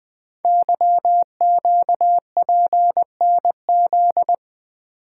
Morse code, heard as a keyed tone, about 20 wpm, 700 Hz. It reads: YQPNZ
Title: Code Groups - 5 Letters - 20wpm